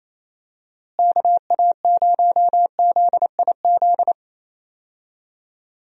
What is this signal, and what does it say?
Morse code, heard as a keyed tone, about 28 wpm, 700 Hz. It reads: KA0ZIZ
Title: USA Call Signs - 6 Characters - Repeated 3x - 28wpm